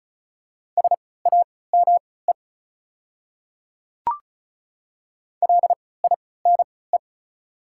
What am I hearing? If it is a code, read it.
Morse code, heard as a keyed tone, about 34 wpm, spaced as for 20 wpm, 690 Hz. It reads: SAME LINE
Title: 3x Character Spacing - Top 500 Words - 35wpm